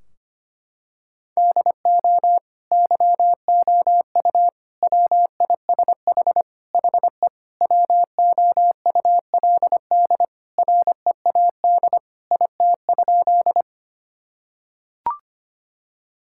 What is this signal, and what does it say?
Morse code, heard as a keyed tone, about 25 wpm, 700 Hz. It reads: DO YOU WISH HE WOULD READ IT?
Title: Sentences from Top 1600 Words 25wpm